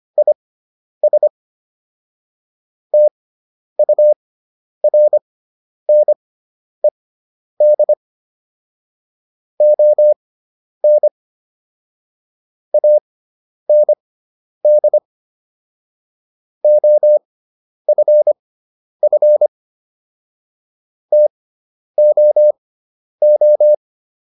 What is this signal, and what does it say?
Morse code, heard as a keyed tone, about 25 wpm, spaced as for 10 wpm, 600 Hz. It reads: IS TURNED ON AND OFF TOO